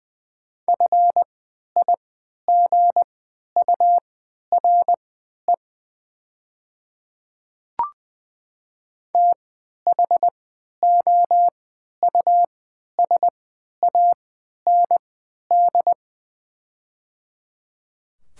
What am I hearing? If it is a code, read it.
Morse code, heard as a keyed tone, about 20 wpm, spaced as for 11 wpm, 700 Hz. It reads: FIGURE THOUSAND
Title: 3x Character Spacing - Top 500 Words - 20wpm